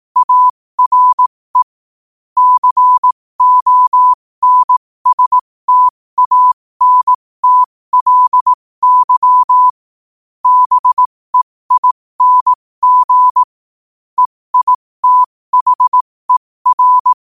Morse code, heard as a keyed tone, about 18 wpm, spaced as for 15 wpm, 1000 Hz. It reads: ARE CONSTANTLY BEING EITHER
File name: qotdfiles / 15wpm